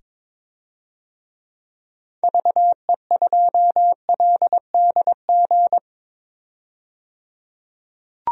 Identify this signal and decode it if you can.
Morse code, heard as a keyed tone, about 22 wpm, 700 Hz. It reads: VE2LDG
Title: DXCC-100 Call Signs - 6 Characters - 22wpm